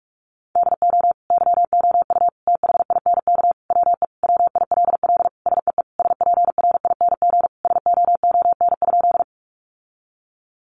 Morse code, heard as a keyed tone, about 44 wpm, 700 Hz. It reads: DO YOU THINK WE WILL SEE SPRING SOON?
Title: Sentences from Top 1200 Words 45wpm